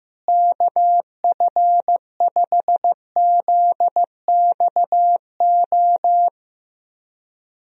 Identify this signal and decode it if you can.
Morse code, heard as a keyed tone, about 15 wpm, 700 Hz. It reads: KF5ZXO
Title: USA Call Signs - 6 Characters - Repeated 3x - 15wpm